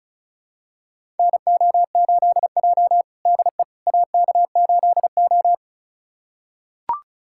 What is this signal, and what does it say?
Morse code, heard as a keyed tone, about 36 wpm, 700 Hz. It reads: NO8J DE AK8O
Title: USA Callsign DE Callsign - 4 Characters - 35wpm